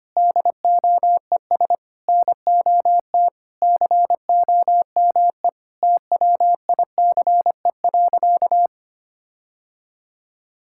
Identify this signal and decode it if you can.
Morse code, heard as a keyed tone, about 25 wpm, 700 Hz. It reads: DOES NOT COME TWICE.
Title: Sentences from Top 1600 Words 25wpm